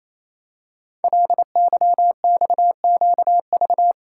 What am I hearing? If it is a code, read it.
Morse code, heard as a keyed tone, about 29 wpm, 690 Hz. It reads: LYXQV
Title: Code Groups - 5 Letters - 28wpm